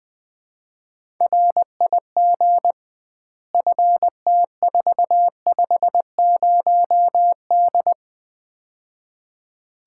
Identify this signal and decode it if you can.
Morse code, heard as a keyed tone, about 20 wpm, 700 Hz. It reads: RIG FT450D
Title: Common QSO Phrases - 2x Word-Spacing - 20wpm